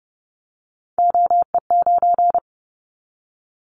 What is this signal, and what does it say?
Morse code, heard as a keyed tone, about 30 wpm, 700 Hz. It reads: OE9
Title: Callsign Prefixes for Top 50 DXCC - 30wpm